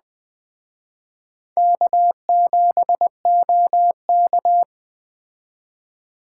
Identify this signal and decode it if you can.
Morse code, heard as a keyed tone, about 21 wpm, 700 Hz. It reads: K7OK
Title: Speed Racing - USA 1x2 Call Signs 17wpm